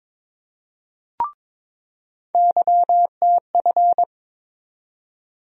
Morse code, heard as a keyed tone, about 22 wpm, 700 Hz. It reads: YTF
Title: USA Call Sign Postfixes - 3 Letters 22wpm